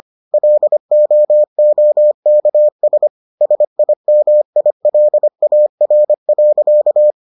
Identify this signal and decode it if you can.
Morse code, heard as a keyed tone, about 25 wpm, 600 Hz. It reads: LOOKS SIMILAR.